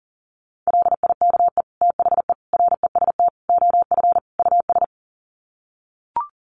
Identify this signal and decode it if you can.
Morse code, heard as a keyed tone, about 40 wpm, 700 Hz. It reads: LIKE THE REST OF US
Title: Sets of 5 Words Encore2 40wpm